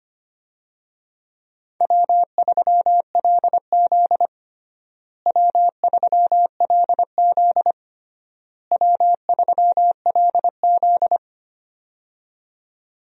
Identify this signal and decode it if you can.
Morse code, heard as a keyed tone, about 25 wpm, 700 Hz. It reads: W3LZ W3LZ W3LZ
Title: USA Call Signs - 4 Characters - Repeated 3x - 25wpm